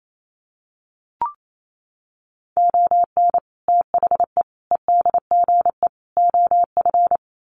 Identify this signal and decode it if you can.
Morse code, heard as a keyed tone, about 28 wpm, 700 Hz. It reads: ON THE EDGE OF
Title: Sets of 4 Words 28wpm